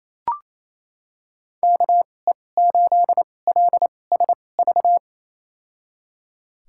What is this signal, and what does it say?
Morse code, heard as a keyed tone, about 28 wpm, spaced as for 20 wpm, 700 Hz. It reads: KE8LSV
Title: USA Call Signs - 2x Character Spacing - 28wpm